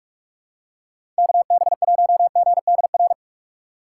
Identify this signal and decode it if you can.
Morse code, heard as a keyed tone, about 46 wpm, 700 Hz. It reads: KB1GDR
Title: USA Call Signs - 6 Characters - 8x Character Spacing - 45wpm